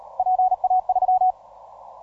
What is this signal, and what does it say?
Morse code, heard as a keyed tone, about 38 wpm, 740 Hz. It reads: PA3